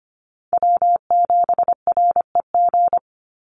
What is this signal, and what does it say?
Morse code, heard as a keyed tone, about 25 wpm, 700 Hz. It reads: W7REG
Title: USA Call Signs 25wpm